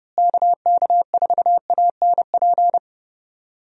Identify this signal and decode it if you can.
Morse code, heard as a keyed tone, about 30 wpm, 700 Hz. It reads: KK4ANP
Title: USA Call Signs - 6 Characters - Repeated 2x - 30wpm